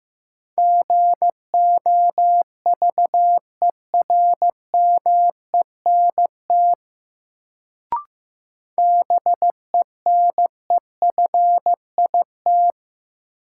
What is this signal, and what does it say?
Morse code, heard as a keyed tone, about 15 wpm, 700 Hz. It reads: GOVERMENT BENEFIT
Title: Sentences from Top 1300 Words - New Word Drill Alt - 15wpm